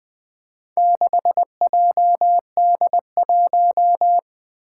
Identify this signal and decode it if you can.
Morse code, heard as a keyed tone, about 20 wpm, 700 Hz. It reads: 6JD1
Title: Four Challenging Letter-Number Combinations 20wpm